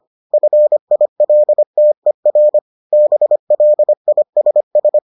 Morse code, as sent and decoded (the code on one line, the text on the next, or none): ..-. .. .-.. - . .-. / -... .-.. .. ... ...
FILTER BLISS